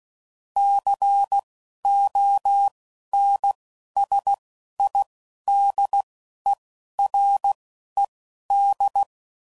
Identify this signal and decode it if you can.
Morse code, heard as a keyed tone, about 16 wpm, spaced as for 11 wpm, 800 Hz. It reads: CONSIDERED
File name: CW Code Groups @10WPM / Technical Subjects / XL